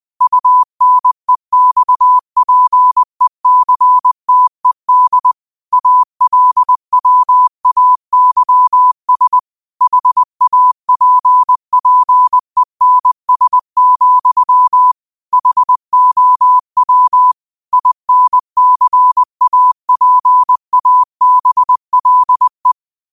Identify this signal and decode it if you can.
Morse code, heard as a keyed tone, about 20 wpm, 1000 Hz. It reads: UNEXPECTED ALWAYS HAPPENS, HOW INCAPABLE